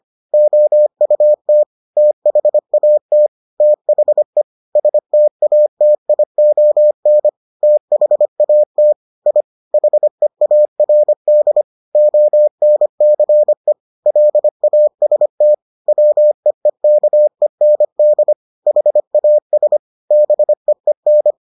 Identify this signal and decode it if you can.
Morse code, heard as a keyed tone, about 25 wpm, 600 Hz. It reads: OUT THAT THE STATION THAT I HEARD ONCE LAST WEEKEND HAS BEEN